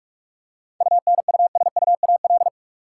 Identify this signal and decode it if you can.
Morse code, heard as a keyed tone, about 44 wpm, 700 Hz. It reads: UNUSUAL
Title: Seven Letter Words - 45wpm